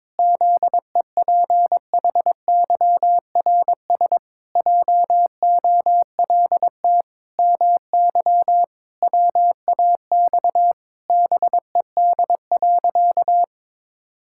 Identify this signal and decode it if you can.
Morse code, heard as a keyed tone, about 22 wpm, 700 Hz. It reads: ZEPHYRS JOLT MY WAX BED.
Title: Pangram Challenge v2 - 22wpm